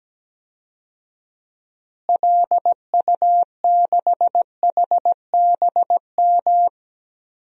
Morse code, heard as a keyed tone, about 17 wpm, 700 Hz. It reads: LU6HBM
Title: DXCC-100 Call Signs - 6 Characters - Repeated 2x - 17wpm